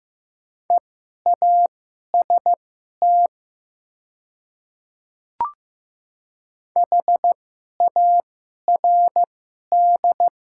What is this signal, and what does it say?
Morse code, heard as a keyed tone, about 15 wpm, spaced as for 11 wpm, 700 Hz. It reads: EAST HARD
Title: 2x Character Spacing - Top 500 Words - 15wpm